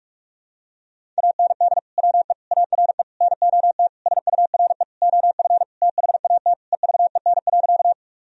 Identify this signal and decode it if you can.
Morse code, heard as a keyed tone, about 46 wpm, 700 Hz. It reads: AND WE ARE NOT SURE OF THAT EVEN.
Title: Sentences from Top 1000 Words - Review Part 1 - 45wpm